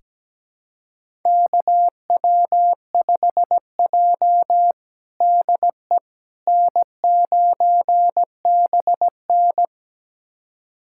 Morse code, heard as a keyed tone, about 17 wpm, 700 Hz. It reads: KW5J DE N9BN
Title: USA Callsign DE Callsign - 4 Characters - 17wpm